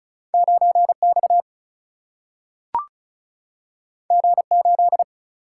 Morse code, heard as a keyed tone, about 36 wpm, 700 Hz. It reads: 9X G8
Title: Single-Letter Single-Number 35wpm